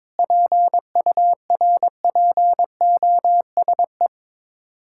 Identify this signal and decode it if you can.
Morse code, heard as a keyed tone, about 22 wpm, 700 Hz. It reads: PURPOSE
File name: Sets of 9 Words - 22wpm